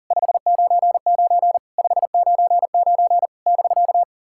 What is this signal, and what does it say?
Morse code, heard as a keyed tone, about 40 wpm, 690 Hz. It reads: 599 599 <BK>
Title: SOTA QSO Practice 40wpm